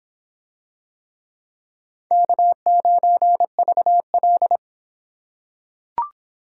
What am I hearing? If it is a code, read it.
Morse code, heard as a keyed tone, about 26 wpm, 700 Hz. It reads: K9VL